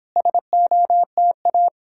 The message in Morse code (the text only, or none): SOTA